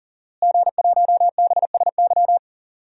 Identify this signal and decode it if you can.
Morse code, heard as a keyed tone, about 40 wpm, 700 Hz. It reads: G1BSY
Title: DXCC-100 Call Signs - 5 Characters - Repeated 3x - 40wpm